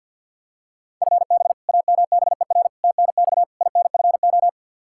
Morse code, heard as a keyed tone, about 53 wpm, 700 Hz. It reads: FB AMBER TNX INFO
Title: Practice QSO - Exchange 3 - More Info - 50wpm